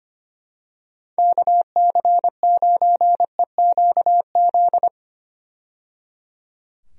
Morse code, heard as a keyed tone, about 25 wpm, 700 Hz. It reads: KC9EQZ
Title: USA Call Signs - 6 Characters - 25wpm